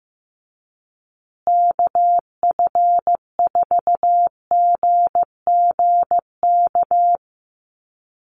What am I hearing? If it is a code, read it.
Morse code, heard as a keyed tone, about 15 wpm, 700 Hz. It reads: KF4GGK
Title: USA Call Signs Repeated 2x 15wpm